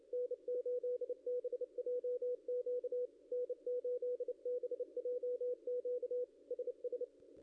Morse code, heard as a keyed tone, about 30 wpm, 490 Hz, with radio noise. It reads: N8BJQ N8BJQ SS
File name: contests / ss-cw-k5tr-op-n5rz -0600